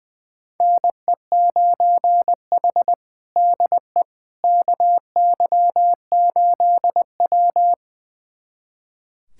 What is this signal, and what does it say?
Morse code, heard as a keyed tone, about 20 wpm, 700 Hz. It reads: NE9H DE KY8W
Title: USA Callsign DE Callsign - 4 Characters - 20wpm